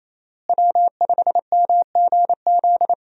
Morse code, heard as a keyed tone, about 28 wpm, 700 Hz. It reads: W5MGZ
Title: USA Call Signs - 5 Characters - Repeated 2x - 28wpm